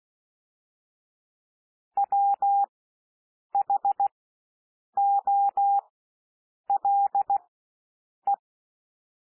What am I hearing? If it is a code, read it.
Morse code, heard as a keyed tone, about 16 wpm, spaced as for 7 wpm, 800 Hz. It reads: WHOLE
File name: SOUND WORDS 16 wpm